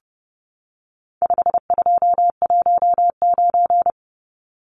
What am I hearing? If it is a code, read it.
Morse code, heard as a keyed tone, about 30 wpm, 700 Hz. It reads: 5219
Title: Numbers - 4-Digit 30wpm